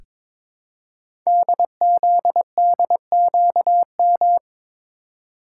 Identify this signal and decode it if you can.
Morse code, heard as a keyed tone, about 23 wpm, 700 Hz. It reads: DZDQM
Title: Code Groups - 5 Letters - 22wpm